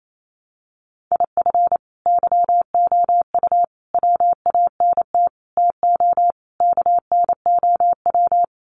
Morse code, heard as a keyed tone, about 28 wpm, 700 Hz. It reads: IF YOU WANT TO KNOW
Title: Sets of 5 Words Encore2 28wpm